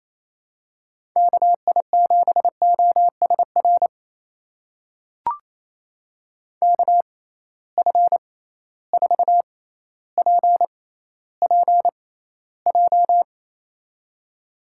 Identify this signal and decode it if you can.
Morse code, heard as a keyed tone, about 28 wpm, spaced as for 14 wpm, 700 Hz. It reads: KI7OSR KF4PPJ
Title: USA Call Signs - 6 Characters - 6x Character Spacing - 28wpm